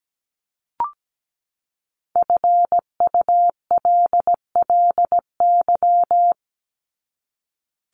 Morse code, heard as a keyed tone, about 17 wpm, 700 Hz. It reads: FULLY
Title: Top 1001-1250 Words 17wpm